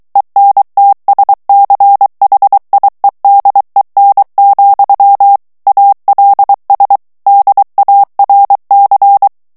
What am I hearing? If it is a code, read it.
Morse code, heard as a keyed tone, about 23 wpm, 800 Hz. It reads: ENTSCHIEDEN, ALS DARC
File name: morse mp3 / ALLEMAND texte clair 1 vitesse 22 mots minute